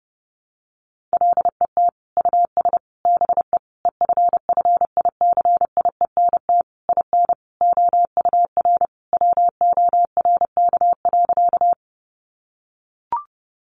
Morse code, heard as a keyed tone, about 30 wpm, 700 Hz. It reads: LET US BE EFFICIENT IN OUR WORK.
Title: Sentences from Top 1300 Words 30wpm